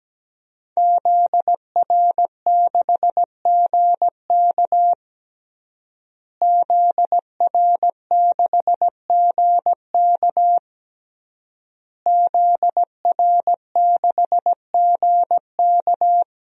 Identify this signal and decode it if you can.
Morse code, heard as a keyed tone, about 17 wpm, 700 Hz. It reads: ZR6GK ZR6GK ZR6GK
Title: DXCC-100 Call Signs - 5 Characters - Repeated 3x - 17wpm